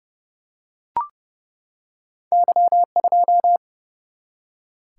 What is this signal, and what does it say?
Morse code, heard as a keyed tone, about 30 wpm, 700 Hz. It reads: Y2
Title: Single-Letter Single-Number 30wpm